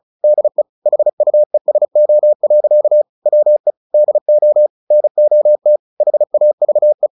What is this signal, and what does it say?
Morse code, heard as a keyed tone, about 34 wpm, 600 Hz. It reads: DE HUESO. WE DO NOT HAVE
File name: the old man and the sea / the old man and the sea 1 35wpm